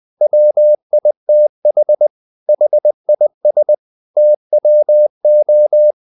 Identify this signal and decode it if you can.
Morse code, heard as a keyed tone, about 20 wpm, 600 Hz. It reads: WITH HIS TWO
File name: the old man and the sea part 2 20wpm